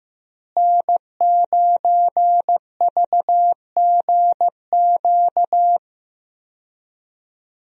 Morse code, heard as a keyed tone, about 15 wpm, 700 Hz. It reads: N9VGQ
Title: USA Call Signs - 5 Characters - Repeated 2x - 15wpm